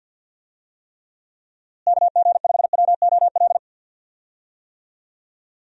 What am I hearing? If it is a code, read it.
Morse code, heard as a keyed tone, about 52 wpm, 700 Hz. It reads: KG5WOL
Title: USA Call Signs - 6 Characters - Repeated 3x - 50wpm